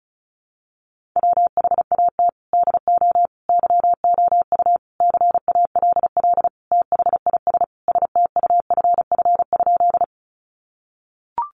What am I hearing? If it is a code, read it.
Morse code, heard as a keyed tone, about 34 wpm, 700 Hz. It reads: WHAT DO YOU CALL THIS STUFF?
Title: Sentences from Top 1400 Words 35wpm